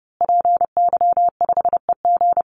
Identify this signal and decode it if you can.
Morse code, heard as a keyed tone, about 30 wpm, 700 Hz. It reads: PY5EG
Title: Call Signs Repeated 2x - 30wpm